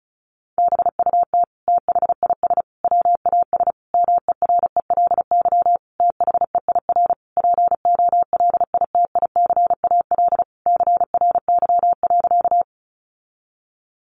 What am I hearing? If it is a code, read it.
Morse code, heard as a keyed tone, about 34 wpm, 700 Hz. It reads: BUT THIS WAS MERELY THEIR POLITICAL CRY.
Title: Sentences from Top 1400 Words 35wpm